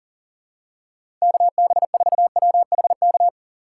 Morse code, heard as a keyed tone, about 40 wpm, 700 Hz. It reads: KB4WHK
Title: USA Call Signs - 6 Characters - 6x Character Spacing - 40wpm